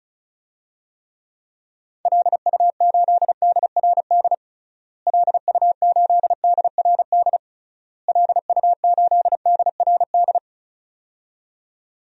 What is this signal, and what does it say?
Morse code, heard as a keyed tone, about 34 wpm, 700 Hz. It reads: LU8DRD LU8DRD LU8DRD
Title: DXCC-100 Call Signs - 6 Characters - Repeated 3x - 35wpm